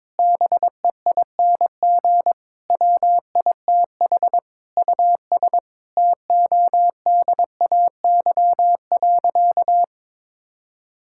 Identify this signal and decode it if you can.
Morse code, heard as a keyed tone, about 22 wpm, 700 Hz. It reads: BEING WITH US TODAY.
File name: practice / 22wpm / Sentences with 8-Word Phrases - 22wpm